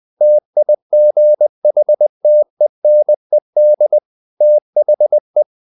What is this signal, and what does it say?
Morse code, heard as a keyed tone, about 20 wpm, 600 Hz. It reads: TIGHTENED THE
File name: the old man and the sea / the old man and the sea part 1 20wpm